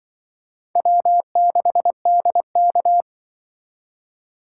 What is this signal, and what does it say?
Morse code, heard as a keyed tone, about 24 wpm, 700 Hz. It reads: W6DK
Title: Speed Racing - USA 1x2 Call Signs 20wpm